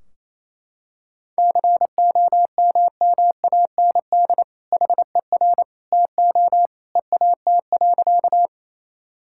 Morse code, heard as a keyed tone, about 28 wpm, 700 Hz. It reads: COMMAND HER TO EAT.
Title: Sentences from Top 1600 Words 28wpm